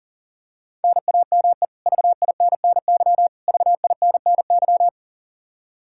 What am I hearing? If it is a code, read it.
Morse code, heard as a keyed tone, about 40 wpm, 700 Hz. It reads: NAME VINNY VINNY
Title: Practice QSO - Exchange 1 - Beginning - 40wpm